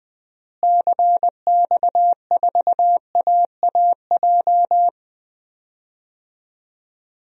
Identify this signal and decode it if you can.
Morse code, heard as a keyed tone, about 20 wpm, 700 Hz. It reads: CX4AAJ
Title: DXCC-100 Call Signs - 6 Characters - Repeated 3x - 20wpm